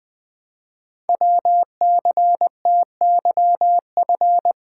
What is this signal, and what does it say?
Morse code, heard as a keyed tone, about 20 wpm, 700 Hz. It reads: WCTYF
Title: Code Groups - 5 Letter:Numbers - 20wpm